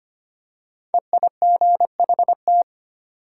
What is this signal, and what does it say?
Morse code, heard as a keyed tone, about 25 wpm, 700 Hz. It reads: EIGHT
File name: 5 Letter Words - First 3 Letters Given - 25wpm